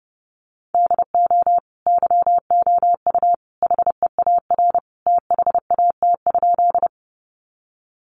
Morse code, heard as a keyed tone, about 28 wpm, 700 Hz. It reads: DO YOU HEAR THAT?
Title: Sentences from Top 300 Words 30wpm